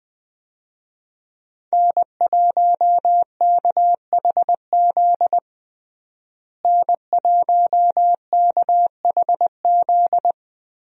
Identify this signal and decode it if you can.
Morse code, heard as a keyed tone, about 20 wpm, 700 Hz. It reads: N1KHZ N1KHZ
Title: USA Call Signs - 5 Characters - Repeated 2x - 20wpm